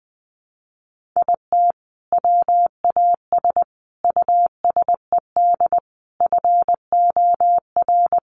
Morse code, heard as a keyed tone, about 19 wpm, 700 Hz. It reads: IT WAS USED FOR
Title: Sentences from Top 1400 Words 20wpm